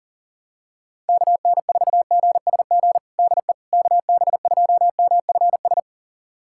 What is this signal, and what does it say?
Morse code, heard as a keyed tone, about 37 wpm, 700 Hz. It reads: KN4GSG DE KB2MFS
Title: USA Callsign DE Callsign - 6 Characters - 40wpm